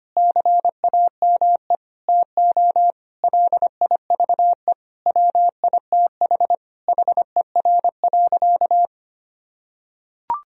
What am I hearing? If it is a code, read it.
Morse code, heard as a keyed tone, about 25 wpm, 700 Hz. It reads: CAME TO LIVE WITH HER.
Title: Sentences from Top 1000 Words - Review Part 3 - 25wpm